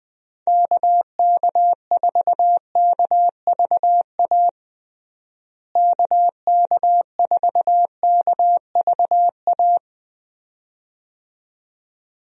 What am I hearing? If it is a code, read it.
Morse code, heard as a keyed tone, about 20 wpm, 700 Hz. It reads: KK4KVA KK4KVA
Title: USA Call Signs - 6 Characters - Repeated 2x - 20wpm